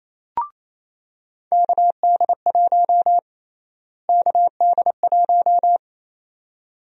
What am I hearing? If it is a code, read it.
Morse code, heard as a keyed tone, about 28 wpm, 700 Hz. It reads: KD1 KD1
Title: USA Call Sign Prefixes - Repeated 2x - 28wpm